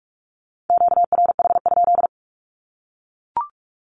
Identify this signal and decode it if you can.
Morse code, heard as a keyed tone, about 44 wpm, 700 Hz. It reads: QRH?
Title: Q-Codes Repeated 3x 45wpm